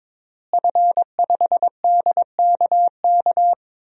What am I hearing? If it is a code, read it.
Morse code, heard as a keyed tone, about 21 wpm, 700 Hz. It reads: F5DKK
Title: DXCC-100 Call Signs - 5 Characters - Repeated 3x - 22wpm